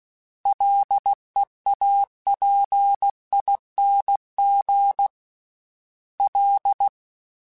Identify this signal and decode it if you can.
Morse code, heard as a keyed tone, about 16 wpm, 800 Hz. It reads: LEAPING L
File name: Combinations / Combinations-mp3s / Combinations@15